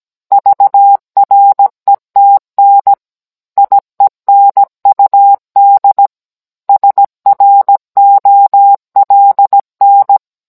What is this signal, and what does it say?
Morse code, heard as a keyed tone, about 17 wpm, 800 Hz. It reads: VRETN IENUD SROLN